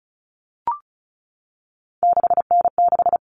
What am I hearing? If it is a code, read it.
Morse code, heard as a keyed tone, about 34 wpm, 700 Hz. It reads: 6N6
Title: All Cut Numbers - 3-Digit 35wpm